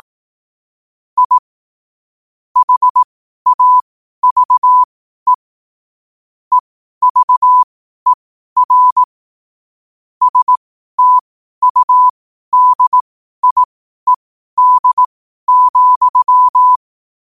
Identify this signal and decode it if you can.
Morse code, heard as a keyed tone, about 18 wpm, spaced as for 12 wpm, 1000 Hz. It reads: I HAVE EVER STUDIED,
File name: qotdfiles / 12wpm